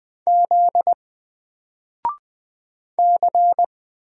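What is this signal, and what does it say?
Morse code, heard as a keyed tone, about 20 wpm, 700 Hz. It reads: Z C
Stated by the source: Single letter and number Morse drill at 20 wpm, from N4LCD.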